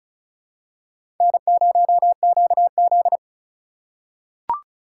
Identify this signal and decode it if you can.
Morse code, heard as a keyed tone, about 36 wpm, 700 Hz. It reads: N0QZ
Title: USA Call Signs - 4 Characters - 2x Character Spacing - 35wpm